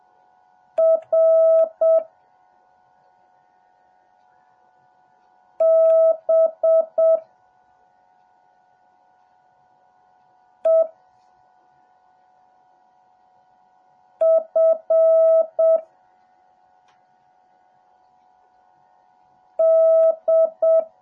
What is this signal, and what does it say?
Morse code, heard as a keyed tone, about 7 wpm, 650 Hz. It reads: R B E F D
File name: jh8cbh / 2021cw / cw08a